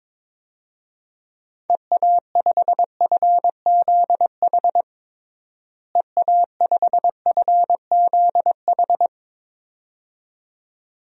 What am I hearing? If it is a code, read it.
Morse code, heard as a keyed tone, about 22 wpm, 700 Hz. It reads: EA5FZH EA5FZH
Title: DXCC-100 Call Signs - 6 Characters - Repeated 2x - 22wpm